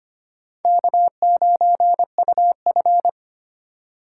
Morse code, heard as a keyed tone, about 25 wpm, 700 Hz. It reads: K9UF
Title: USA Call Signs - 3x Character Spacing - 25wpm